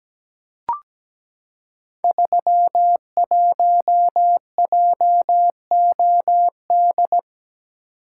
Morse code, heard as a keyed tone, about 17 wpm, 700 Hz. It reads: 31JOD
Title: Code Groups - 5 Letter:Numbers - 17wpm